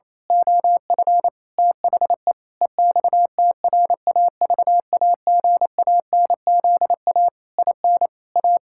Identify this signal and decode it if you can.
Morse code, heard as a keyed tone, about 28 wpm, 700 Hz. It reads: OF THE EXTRAVAGANZA IN A
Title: Pangram Challenge v2 - 28wpm